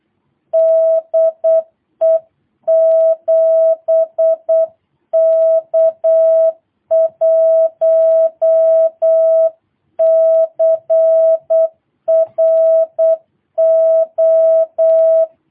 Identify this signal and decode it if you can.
Morse code, heard as a keyed tone, about 8 wpm, 650 Hz. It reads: DE 7K1CRO
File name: jh8cbh / 2022cw